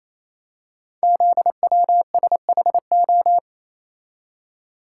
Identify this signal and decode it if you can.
Morse code, heard as a keyed tone, about 28 wpm, 700 Hz. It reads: ZWSHO
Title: Code Groups - 5 Letters - 28wpm